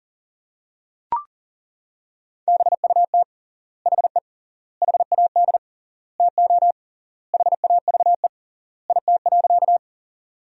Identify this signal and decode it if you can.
Morse code, heard as a keyed tone, about 40 wpm, 700 Hz. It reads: BUT HE HAD TO HAVE IT.
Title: 3x Word-Spacing - Sentences from Top 100 Words - 40wpm